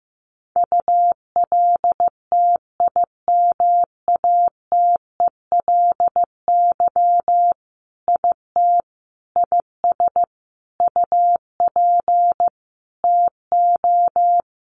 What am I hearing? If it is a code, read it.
Morse code, heard as a keyed tone, about 15 wpm, 700 Hz. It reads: ULTIMATELY IT IS UP TO
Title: Sentences from Top 1200 Words 15wpm